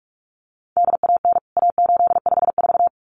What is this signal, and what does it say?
Morse code, heard as a keyed tone, about 44 wpm, 700 Hz. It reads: DAN A854
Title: CWT Exchange 45wpm